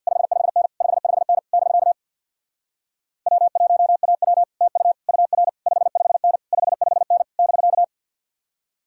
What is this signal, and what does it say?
Morse code, heard as a keyed tone, about 64 wpm, 700 Hz. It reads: 55N 55N <BK> W1AW TU UR 55N 55N <BK>